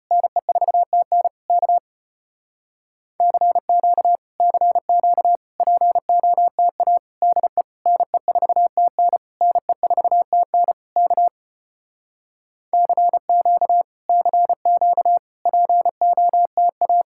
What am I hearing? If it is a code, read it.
Morse code, heard as a keyed tone, about 33 wpm, 700 Hz. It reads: NE4TN K CQ CQ POTA DE NE4TN NE4TN K CQ CQ POTA